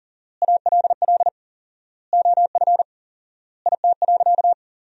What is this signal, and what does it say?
Morse code, heard as a keyed tone, about 40 wpm, 700 Hz. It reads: ALL OF IT.
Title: 4x Word-Spacing - Sentences from Top 500 Words - 40wpm